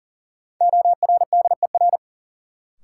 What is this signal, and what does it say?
Morse code, heard as a keyed tone, about 40 wpm, 700 Hz. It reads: ORDER
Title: Sentences from Top 400 Words - New Word Drill - 40wpm